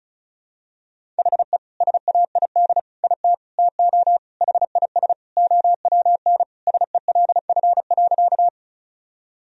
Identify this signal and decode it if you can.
Morse code, heard as a keyed tone, about 34 wpm, 690 Hz. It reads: HE SAID IT TO HIS OWN SELF.
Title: Sentences from Top 1000 Words - Review Part 1 - 35wpm